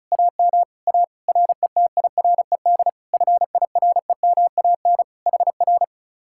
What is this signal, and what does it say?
Morse code, heard as a keyed tone, about 37 wpm, 700 Hz. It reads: AM A RETIRED FIREMAN HR
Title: Speed-Racing Common QSO Phrases - 30wpm